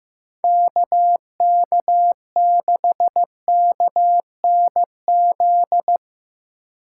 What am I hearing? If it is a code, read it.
Morse code, heard as a keyed tone, about 15 wpm, 700 Hz. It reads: KK6KNZ
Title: USA Call Signs - 6 Characters - 2x Character Spacing - 15wpm